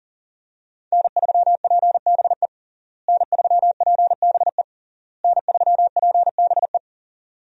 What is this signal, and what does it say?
Morse code, heard as a keyed tone, about 40 wpm, 700 Hz. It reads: N3PBE N3PBE N3PBE
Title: USA Call Signs - 5 Characters - Repeated 3x - 40wpm